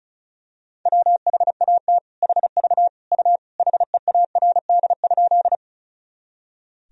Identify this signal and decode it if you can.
Morse code, heard as a keyed tone, about 36 wpm, 700 Hz. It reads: WHAT HV U HEARD?
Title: Sentences from Top 400 Words and Abbreviations 35wpm